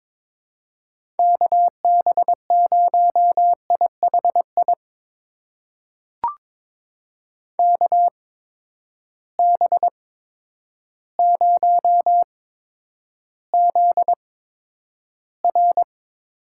Morse code, heard as a keyed tone, about 22 wpm, spaced as for 10 wpm, 700 Hz. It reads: KB0IHI KB0ZR
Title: USA Call Signs - 6 Characters - 8x Character Spacing - 22wpm